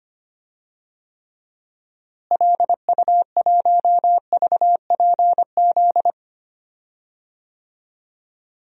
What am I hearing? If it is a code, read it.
Morse code, heard as a keyed tone, about 25 wpm, 700 Hz. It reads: LU1VPZ
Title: DXCC-100 Call Signs - 6 Characters - Repeated 2x - 25wpm